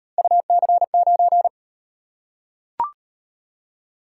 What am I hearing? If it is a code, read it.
Morse code, heard as a keyed tone, about 38 wpm, 700 Hz. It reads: UC9